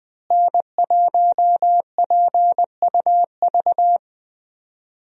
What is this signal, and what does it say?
Morse code, heard as a keyed tone, about 21 wpm, 700 Hz. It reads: N1PUV
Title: USA Call Signs - 5 Characters - Repeated 2x - 20wpm